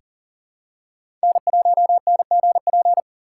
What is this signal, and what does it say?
Morse code, heard as a keyed tone, about 39 wpm, 700 Hz. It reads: N1NGP
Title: USA Call Signs - 5 Characters - 2x Character Spacing - 40wpm